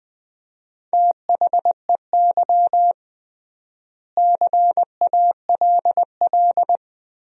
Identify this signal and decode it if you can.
Morse code, heard as a keyed tone, about 21 wpm, 700 Hz. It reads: THEY CALL
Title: 3x Word-Spacing - Sentences from Top 500 Words - 20wpm